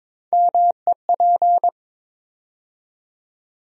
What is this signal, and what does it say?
Morse code, heard as a keyed tone, about 22 wpm, 700 Hz. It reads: MEP
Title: USA Call Sign Postfixes - 3 Letters 22wpm